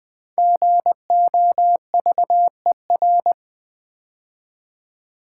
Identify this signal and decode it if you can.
Morse code, heard as a keyed tone, about 20 wpm, 700 Hz. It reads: GOVER
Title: Build A Word - 5 Letters At A Time - 10-Letter Words - 20wpm